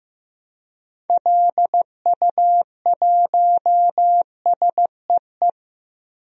Morse code, heard as a keyed tone, about 15 wpm, 700 Hz. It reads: LU1SEE
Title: DXCC-100 Call Signs - 6 Characters - 15wpm